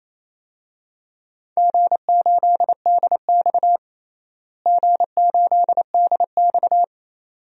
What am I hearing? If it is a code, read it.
Morse code, heard as a keyed tone, about 28 wpm, 700 Hz. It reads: G8DX G8DX
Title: DXCC-100 Call Signs - 4 Characters - Repeated 2x - 28wpm